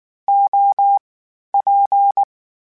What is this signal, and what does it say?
Morse code, heard as a keyed tone, about 19 wpm, 800 Hz. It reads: O P
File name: Variable Spacing / Variable Spacing-mp3s / TOs @18